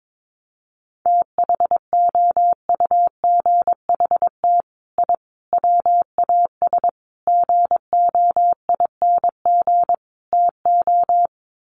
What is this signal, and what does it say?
Morse code, heard as a keyed tone, about 22 wpm, 700 Hz. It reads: THOUGHT I WAS GOING TO
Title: Sets of 5 Words Encore2 22wpm